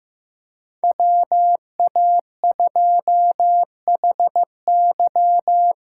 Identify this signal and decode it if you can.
Morse code, heard as a keyed tone, about 15 wpm, 700 Hz. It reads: WA2HY
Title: USA Call Signs - 6 Characters - 8x Character Spacing - 15wpm